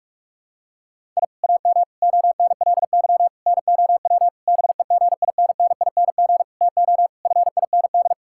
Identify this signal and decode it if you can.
Morse code, heard as a keyed tone, about 47 wpm, 700 Hz. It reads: I AM ONLY NOW BEGINNING TO FIND
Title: Sentences from Top 1000 Words - Review Part 5 - 45wpm